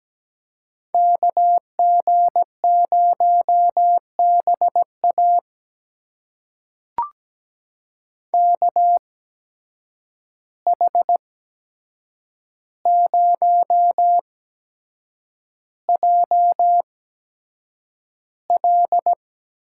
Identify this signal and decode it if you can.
Morse code, heard as a keyed tone, about 17 wpm, spaced as for 7 wpm, 700 Hz. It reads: KG0BA KH0JL
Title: USA Call Signs - 5 Characters - 8x Character Spacing - 17wpm